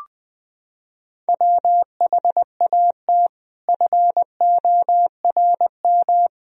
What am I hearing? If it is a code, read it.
Morse code, heard as a keyed tone, about 20 wpm, 700 Hz. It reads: WHAT FORM
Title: Sentences from Top 1000 Words - Review Part 1 - 20wpm